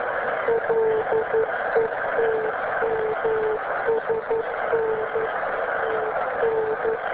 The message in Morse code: none